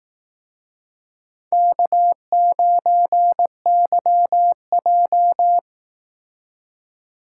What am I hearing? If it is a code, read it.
Morse code, heard as a keyed tone, about 18 wpm, 700 Hz. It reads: K9YJ
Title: Speed Racing - USA 1x2 Call Signs 15wpm